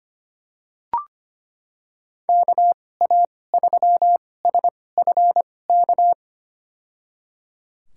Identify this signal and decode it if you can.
Morse code, heard as a keyed tone, about 25 wpm, spaced as for 18 wpm, 700 Hz. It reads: KA3SFK
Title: USA Call Signs - 6 Characters - 2x Character Spacing - 25wpm